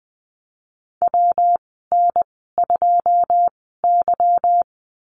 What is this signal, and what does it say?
Morse code, heard as a keyed tone, about 20 wpm, spaced as for 15 wpm, 700 Hz. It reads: WN2Y
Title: 161 Lesson 26g - Letter g - Call Signs 20wpm